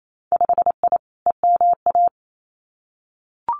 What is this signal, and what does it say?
Morse code, heard as a keyed tone, about 28 wpm, 700 Hz. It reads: 5I EMA
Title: Winter Field Day Exchanges Encore 28wpm